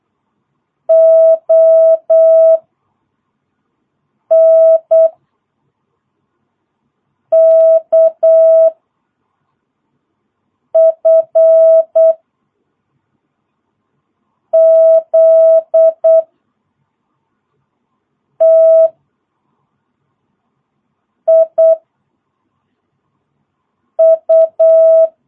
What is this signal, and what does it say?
Morse code, heard as a keyed tone, about 8 wpm, 660 Hz. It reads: O N K F Z T I U
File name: jh8cbh / 2022cw